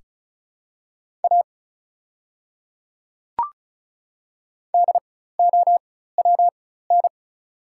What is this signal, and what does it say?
Morse code, heard as a keyed tone, about 36 wpm, spaced as for 16 wpm, 690 Hz. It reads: A DOWN
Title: 4x Character Spacing - Top 100 Words - 35wpm